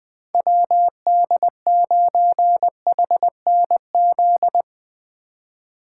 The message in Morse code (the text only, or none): WD9HNZ